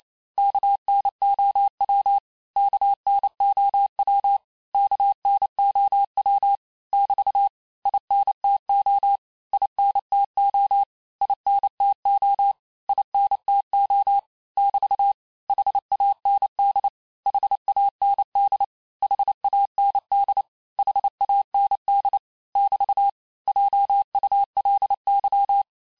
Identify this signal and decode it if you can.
Morse code, heard as a keyed tone, about 28 wpm, 800 Hz. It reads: KNOW KNOW KNOW <BT> INTO INTO INTO INTO <BT> HAND HAND HAND HAND <BT> JULY
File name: cours-du-mercredi / audios / cours-865-teste-les-mots-vitesse-27mpm